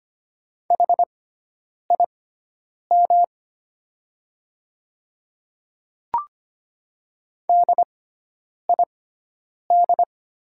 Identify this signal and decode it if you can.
Morse code, heard as a keyed tone, about 25 wpm, spaced as for 9 wpm, 690 Hz. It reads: HIM DID
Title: 6x Character Spacing - Top 100 Words - 25wpm